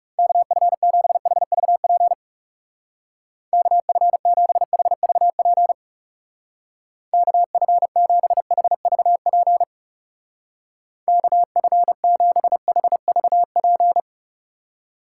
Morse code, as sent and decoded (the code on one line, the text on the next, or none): -.- ..-. --... .... ...- .--. / -.- ..-. --... .... ...- .--. / -.- ..-. --... .... ...- .--. / -.- ..-. --... .... ...- .--.
KF7HVP KF7HVP KF7HVP KF7HVP